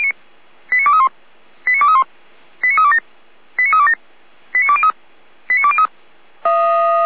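Five-tone selective call (ZVEI), heard as quick runs of five tones, tones of about 80 ms. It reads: 89231 89231 89327 89327 89203 89203
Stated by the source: ZVEI five-tone selective call received on SDR.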